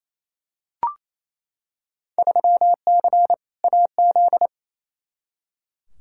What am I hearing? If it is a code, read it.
Morse code, heard as a keyed tone, about 28 wpm, 700 Hz. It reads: 3C AZ
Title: Field Day Exchanges Encore - 28wpm